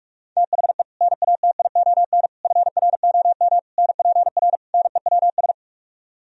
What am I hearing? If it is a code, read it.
Morse code, heard as a keyed tone, about 45 wpm, 700 Hz. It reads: THE NATION FROM NPR NEWS